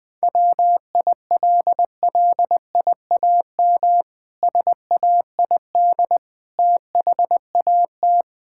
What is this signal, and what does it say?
Morse code, heard as a keyed tone, about 20 wpm, 700 Hz. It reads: WILLIAM SAID THAT
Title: Pangram Challenge v2 - 20wpm